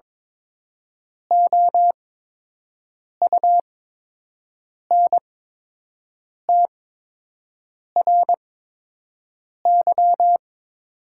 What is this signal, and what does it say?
Morse code, heard as a keyed tone, about 22 wpm, spaced as for 6 wpm, 700 Hz. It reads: OUNTRY
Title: 8x Character Spacing - Top 300 Words - 22wpm